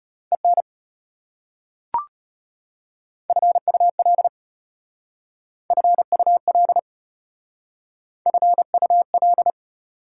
Morse code, heard as a keyed tone, about 34 wpm, 690 Hz. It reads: EN FUL FUL FUL